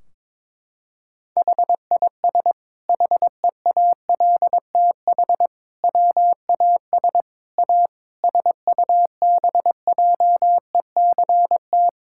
Morse code, heard as a keyed tone, about 22 wpm, 700 Hz. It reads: HIS HEALTH WAS A SUBJECT
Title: Sentences from Top 1600 Words 22wpm